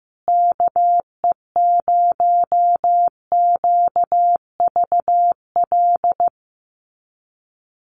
Morse code, heard as a keyed tone, about 15 wpm, 700 Hz. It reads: KE0QVL
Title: USA Call Signs Repeated 2x 15wpm